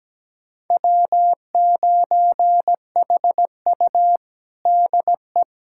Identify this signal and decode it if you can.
Morse code, heard as a keyed tone, about 17 wpm, 700 Hz. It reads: W9HU DE
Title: USA Callsign DE Callsign - 4 Characters - 17wpm